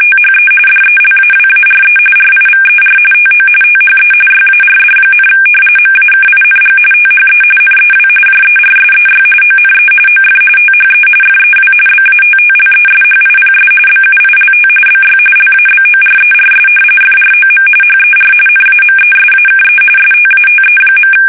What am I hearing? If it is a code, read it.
RTTY, about 45.45 baud. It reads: THE DELAWARE RIVER TO THE YARD, MAKING THE NEW JERSEY ONE OF THE FEW BATTLESHIPS BUILT BY LABORERS FROM HER NAMESAKE STATE.
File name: fsk850 / battleshipnj